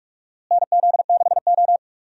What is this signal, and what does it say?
Morse code, heard as a keyed tone, about 46 wpm, 700 Hz. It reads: NZ6O
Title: USA Call Signs - 4 Characters - Repeated 3x - 45wpm